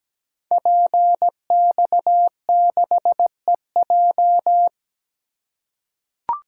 Morse code, heard as a keyed tone, about 17 wpm, 700 Hz. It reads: PX6EJ
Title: Code Groups - 5 Letter:Numbers - 17wpm